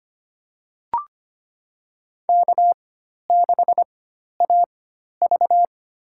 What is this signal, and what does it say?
Morse code, heard as a keyed tone, about 25 wpm, spaced as for 12 wpm, 700 Hz. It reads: K6AV
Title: USA Call Signs - 4 Characters - 4x Character Spacing - 25wpm